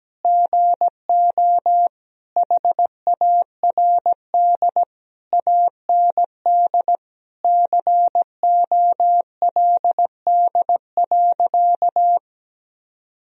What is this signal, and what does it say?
Morse code, heard as a keyed tone, about 17 wpm, 700 Hz. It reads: GO HARD AND COLD.
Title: Sentences from Top 1000 Words - Review Part 1 - 17wpm